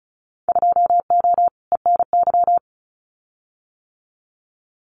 Morse code, heard as a keyed tone, about 36 wpm, 700 Hz. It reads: 2O ENY
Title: Winter Field Day Exchanges 35wpm